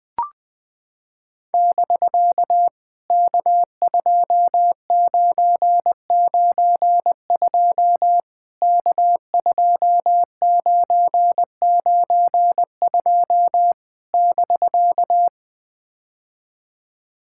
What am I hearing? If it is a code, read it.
Morse code, heard as a keyed tone, about 21 wpm, 700 Hz. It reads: <BK> K2992 K2992 <BK>
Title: POTA QSO Practice - 20wpm